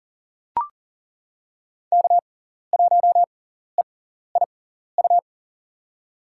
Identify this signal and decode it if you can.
Morse code, heard as a keyed tone, about 37 wpm, spaced as for 14 wpm, 700 Hz. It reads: K1EIU
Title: USA Call Signs - 5 Characters - 6x Character Spacing - 40wpm